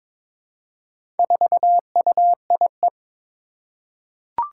Morse code, heard as a keyed tone, about 22 wpm, 700 Hz. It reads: 4UIE
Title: Four Challenging Letter-Number Combinations 22wpm